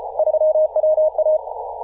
Morse code, heard as a keyed tone, about 34 wpm, 620 Hz. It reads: 3WA
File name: posts / 2017-aug-07-0652z-ri0li-14-036mhz-cw